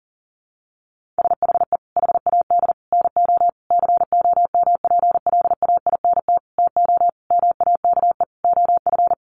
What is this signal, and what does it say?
Morse code, heard as a keyed tone, about 37 wpm, 700 Hz. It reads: SHE HAD NO COMPLAINT TO MAKE OF
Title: Sentences from Top 1400 Words 40wpm